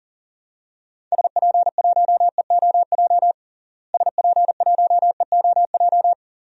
Morse code, heard as a keyed tone, about 40 wpm, 700 Hz. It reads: SP1EOJ SP1EOJ
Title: DXCC-100 Call Signs - 6 Characters - Repeated 3x - 40wpm